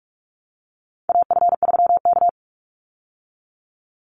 Morse code, heard as a keyed tone, about 44 wpm, 700 Hz. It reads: AF3K
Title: Call Signs Repeated 2x - 45wpm